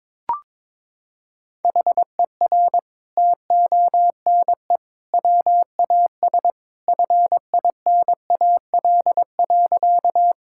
Morse code, heard as a keyed tone, about 22 wpm, 700 Hz. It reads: HER TONE WAS FINAL.
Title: Sentences from Top 1600 Words 22wpm